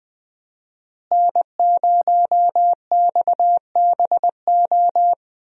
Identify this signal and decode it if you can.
Morse code, heard as a keyed tone, about 20 wpm, 700 Hz. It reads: N0XBO
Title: USA Call Signs - 5 Characters - Repeated 2x - 20wpm